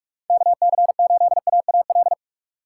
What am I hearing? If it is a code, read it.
Morse code, heard as a keyed tone, about 47 wpm, 700 Hz. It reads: KC8AAL
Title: USA Call Signs - 6 Characters - 4x Character Spacing - 45wpm